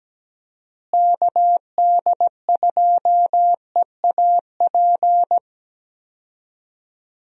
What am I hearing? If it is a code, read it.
Morse code, heard as a keyed tone, about 17 wpm, 700 Hz. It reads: KD2EAP
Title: USA Call Signs - 6 Characters - Repeated 2x - 17wpm